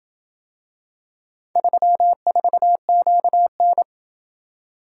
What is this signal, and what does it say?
Morse code, heard as a keyed tone, about 27 wpm, 700 Hz. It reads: 34QN